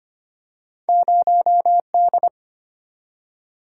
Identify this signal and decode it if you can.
Morse code, heard as a keyed tone, about 25 wpm, 690 Hz. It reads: 0D
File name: Two Letter-Number Combinations - 25wpm